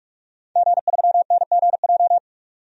Morse code, heard as a keyed tone, about 44 wpm, 700 Hz. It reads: G3NGJ
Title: DXCC-100 Call Signs - 5 Characters - 45wpm